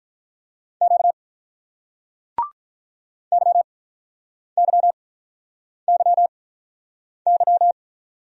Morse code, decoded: Q YYYY

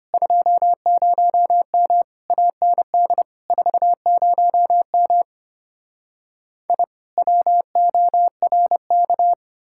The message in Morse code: 20M AND 40M I WORK

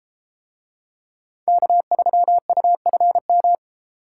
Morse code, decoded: K3UFM

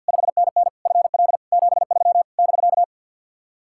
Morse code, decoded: 5NN FL 73 <BK>

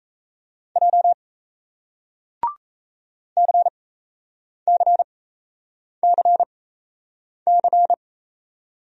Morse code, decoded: J CCCC

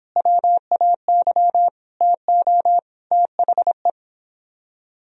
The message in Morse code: WAY TO THE

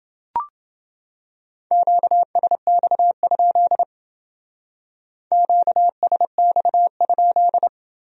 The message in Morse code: QSX? QSX?

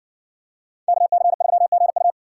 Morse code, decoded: KC3GU

